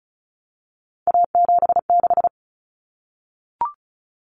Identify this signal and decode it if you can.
Morse code, heard as a keyed tone, about 34 wpm, 700 Hz. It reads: A76
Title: Common Cut Numbers - 3-Digit 35wpm